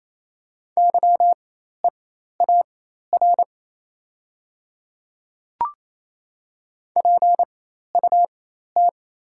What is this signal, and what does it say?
Morse code, heard as a keyed tone, about 28 wpm, spaced as for 13 wpm, 700 Hz. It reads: YEAR PUT
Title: 4x Character Spacing - Top 300 Words - 28wpm